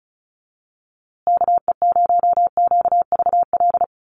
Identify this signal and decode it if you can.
Morse code, heard as a keyed tone, about 34 wpm, 700 Hz. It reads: KE0QVL
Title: USA Call Signs Repeated 2x 35wpm